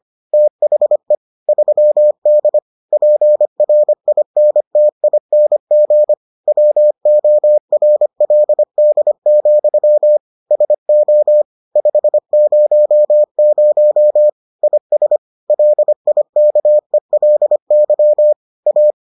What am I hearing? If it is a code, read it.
Morse code, heard as a keyed tone, about 25 wpm, 600 Hz. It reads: THE 3D PRINTING WORLD, SO 500 IS LIKELY A